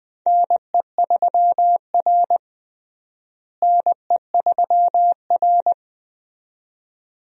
Morse code, heard as a keyed tone, about 20 wpm, 700 Hz. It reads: NE3R NE3R
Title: USA Call Signs - 4 Characters - Repeated 2x - 20wpm